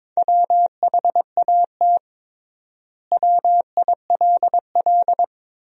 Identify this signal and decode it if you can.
Morse code, heard as a keyed tone, about 22 wpm, 700 Hz. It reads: WHAT WILL
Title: 3x Word-Spacing - Sentences from Top 200 Words - 22wpm